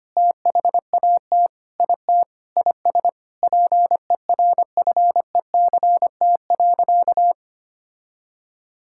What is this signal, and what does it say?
Morse code, heard as a keyed tone, about 25 wpm, 700 Hz. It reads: THAT IT IS PERFECT.
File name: practice / 25wpm / Sentences with 9-Word Phrases - 25wpm